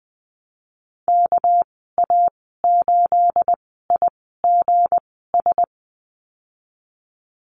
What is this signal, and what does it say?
Morse code, heard as a keyed tone, about 20 wpm, 700 Hz. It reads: K A 8 I G S
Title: 238 Lesson 38g - Letter q - Call Signs 20wpm